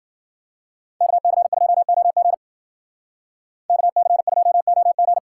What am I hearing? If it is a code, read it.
Morse code, heard as a keyed tone, about 48 wpm, 700 Hz. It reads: KC2OG KC2OG
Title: Speed Racing - USA 2x2 Call Signs 40wpm